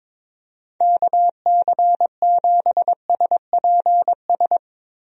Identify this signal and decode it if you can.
Morse code, heard as a keyed tone, about 23 wpm, 700 Hz. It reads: KC7SPS
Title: USA Call Signs - 6 Characters - Repeated 3x - 22wpm